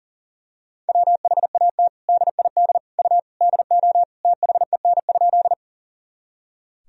Morse code, heard as a keyed tone, about 40 wpm, 700 Hz. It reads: WHAT DID U DO THEN?
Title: Sentences from Top 100 Words and Abbreviations 40wpm